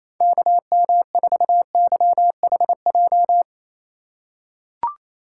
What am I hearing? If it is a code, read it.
Morse code, heard as a keyed tone, about 28 wpm, 700 Hz. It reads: KM4YHJ
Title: USA Call Signs - 8x Character Spacing - 28wpm